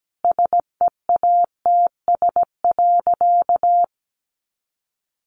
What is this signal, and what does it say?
Morse code, heard as a keyed tone, about 17 wpm, 700 Hz. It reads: SEATS.
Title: Additional Easy Phrases 17wpm